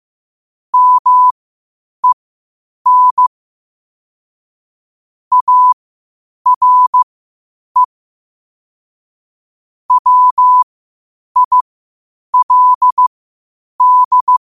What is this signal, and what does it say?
Morse code, heard as a keyed tone, about 15 wpm, spaced as for 8 wpm, 1000 Hz. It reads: MEN ARE WILD
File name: qotdfiles / 8wpm